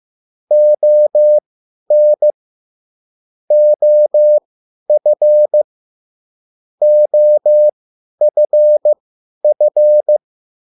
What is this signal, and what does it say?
Morse code, heard as a keyed tone, about 15 wpm, spaced as for 10 wpm, 600 Hz. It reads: ON OF OFF